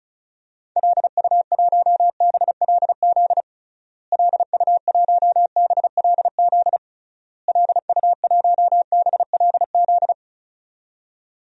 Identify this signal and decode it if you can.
Morse code, heard as a keyed tone, about 36 wpm, 700 Hz. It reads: LU1BLZ LU1BLZ LU1BLZ
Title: DXCC-100 Call Signs - 6 Characters - Repeated 3x - 35wpm